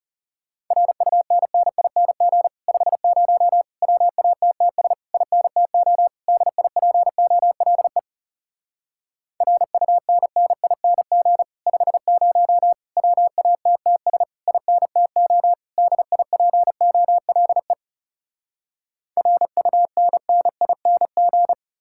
Morse code, heard as a keyed tone, about 40 wpm, 700 Hz. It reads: RUNNING 50 WATTS INTO DIPOLE RUNNING 50 WATTS INTO DIPOLE RUNNING